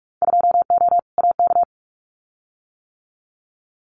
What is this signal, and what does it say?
Morse code, heard as a keyed tone, about 44 wpm, 700 Hz. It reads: 2O AK
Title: Winter Field Day Exchanges Encore 45wpm